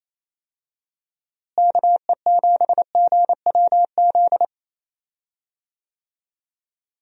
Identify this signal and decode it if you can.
Morse code, heard as a keyed tone, about 28 wpm, 700 Hz. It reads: KE7GWZ
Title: USA Call Signs - 6 Characters - Repeated 3x - 28wpm